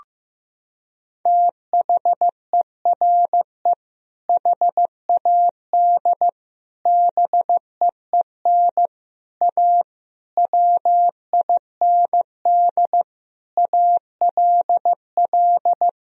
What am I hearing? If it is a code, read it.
Morse code, heard as a keyed tone, about 16 wpm, 700 Hz. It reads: THERE HAD BEEN A WIND ALL
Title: Sentences from Top 1000 Words - Review Part 2 - 15wpm